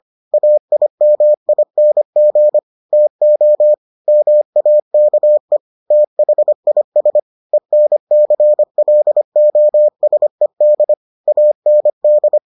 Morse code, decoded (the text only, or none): AIMING TO MAKE THIS ENCLOSED AND